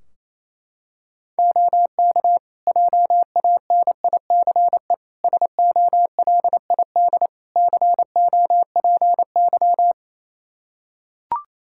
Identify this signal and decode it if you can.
Morse code, heard as a keyed tone, about 28 wpm, 700 Hz. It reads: OK JANICE SOLID COPY
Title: Practice QSO - Exchange 3 - More Info - 28wpm